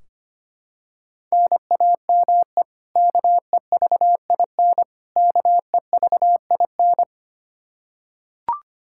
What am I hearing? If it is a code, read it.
Morse code, heard as a keyed tone, about 25 wpm, 700 Hz. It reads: NAME KEVIN KEVIN
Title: Practice QSO - Exchange 1 - Beginning - 25wpm